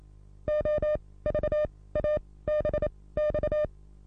Morse code, heard as a keyed tone, about 27 wpm, spaced as for 18 wpm, 600 Hz. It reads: OVABX
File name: repos / Morsekurs / DJ4UF Morsen27